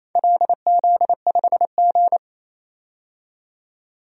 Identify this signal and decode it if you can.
Morse code, heard as a keyed tone, about 28 wpm, 700 Hz. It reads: LZ5G
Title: DXCC-100 Call Signs - 4 Characters - 28wpm